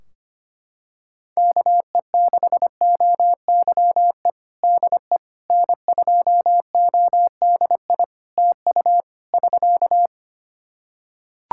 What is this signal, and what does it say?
Morse code, heard as a keyed tone, about 25 wpm, 700 Hz. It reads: KE6OYE DE N2ODI TU <SK>
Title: K1USN SST QSO Practice 25wpm